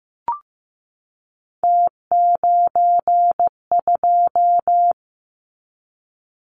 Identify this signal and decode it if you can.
Morse code, heard as a keyed tone, about 15 wpm, 700 Hz. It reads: T92
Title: Callsign Prefixes for Top 50 DXCC - 15wpm